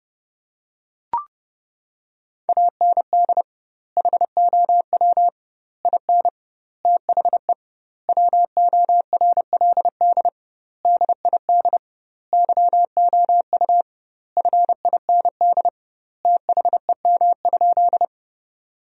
Morse code, heard as a keyed tone, about 30 wpm, 700 Hz. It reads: AND HOW IN THE WORLD DID YOU FIND THEM?
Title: 2x Word-Spacing - Sentences from Top 200 Words - 30wpm